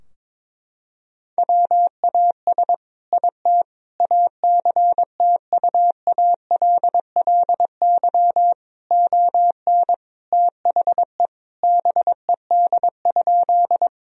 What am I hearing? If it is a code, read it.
Morse code, heard as a keyed tone, about 22 wpm, 700 Hz. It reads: WAS IT ACTUALLY ON THE BED?
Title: Sentences from Top 1000 Words - Review Part 4 - 22wpm